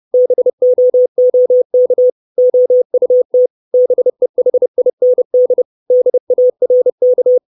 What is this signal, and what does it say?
Morse code, heard as a keyed tone, about 30 wpm, 500 Hz. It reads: BOOK OUT BEHIND DARK
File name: 632 Words 30WPM